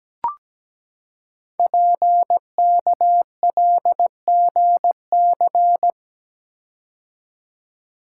Morse code, heard as a keyed tone, about 17 wpm, 700 Hz. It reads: PKLGC
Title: Code Groups - 5 Letter:Numbers - 17wpm